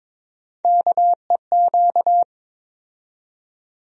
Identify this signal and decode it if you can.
Morse code, heard as a keyed tone, about 21 wpm, 700 Hz. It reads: KEQ
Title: USA Call Sign Postfixes - 3 Letters 22wpm